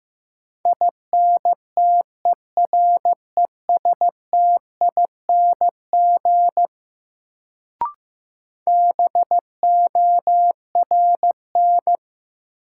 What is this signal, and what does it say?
Morse code, heard as a keyed tone, about 15 wpm, 700 Hz. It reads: INTERESTING BORN
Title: Sentences from Top 1000 Words - New Word Drill - 15wpm